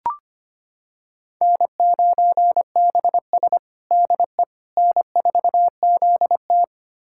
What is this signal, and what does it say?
Morse code, heard as a keyed tone, about 25 wpm, 700 Hz. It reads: N9BS DE N4ZT
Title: USA Callsign DE Callsign - 4 Characters - 25wpm